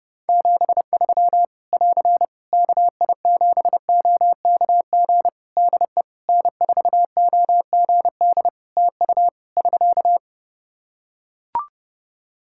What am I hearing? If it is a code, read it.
Morse code, heard as a keyed tone, about 30 wpm, 700 Hz. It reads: 73 <AR> KI7OKG DE N4OGD TU <SK>
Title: K1USN SST QSO Practice 30wpm